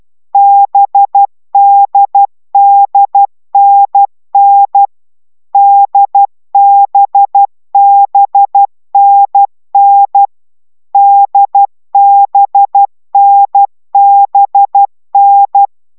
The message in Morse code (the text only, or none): BDDNN DBBNN DBNBN